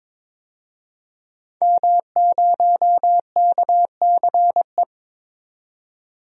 Morse code, heard as a keyed tone, about 22 wpm, 700 Hz. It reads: M0KCE
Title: DXCC-100 Call Signs - 5 Characters - 22wpm